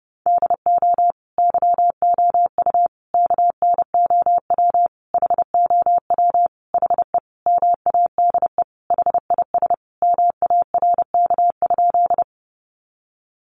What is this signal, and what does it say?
Morse code, heard as a keyed tone, about 30 wpm, 700 Hz. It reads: DO YOU KNOW HOW HE MADE HIS MARK?
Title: Sentences from Top 300 Words 30wpm